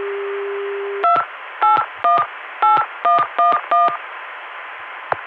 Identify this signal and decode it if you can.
DTMF, heard as telephone touch tones, tones of about 190 ms. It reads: 2818111